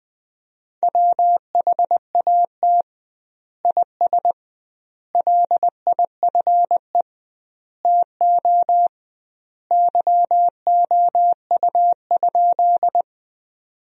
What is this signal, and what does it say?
Morse code, heard as a keyed tone, about 19 wpm, 700 Hz. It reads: WHAT IS LIFE TO YOU?
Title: 2x Word-Spacing - Sentences from Top 300 Words - 20wpm